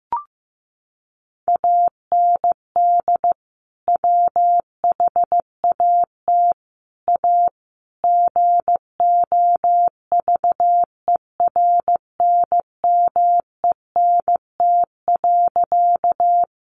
Morse code, heard as a keyed tone, about 15 wpm, 700 Hz. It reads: AND WHAT A GOVERNMENT.
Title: Sentences from Top 600 Words 15wpm